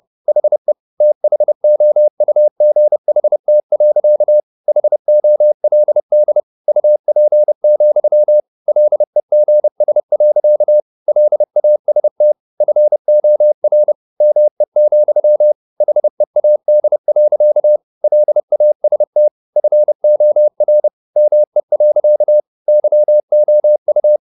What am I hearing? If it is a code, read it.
Morse code, heard as a keyed tone, about 30 wpm, 600 Hz. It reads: HE THOUGHT. HOLD UP, LEGS. LAST FOR ME, HEAD. LAST FOR ME. YOU